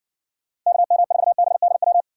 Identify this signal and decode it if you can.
Morse code, heard as a keyed tone, about 59 wpm, 700 Hz. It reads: KM4BDW